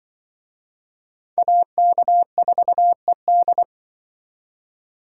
Morse code, heard as a keyed tone, about 25 wpm, 700 Hz. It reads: AK4ED
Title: Speed Racing - USA 2x2 Call Signs 20wpm